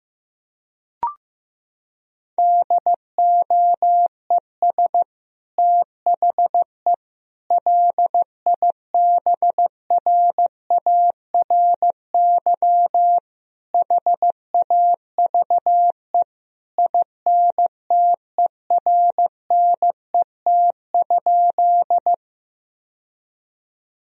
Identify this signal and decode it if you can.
Morse code, heard as a keyed tone, about 15 wpm, 700 Hz. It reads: DOES THE LIBRARY HAVE INTERNET?
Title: Sentences from Top 1000 Words - Review Part 4 - 15wpm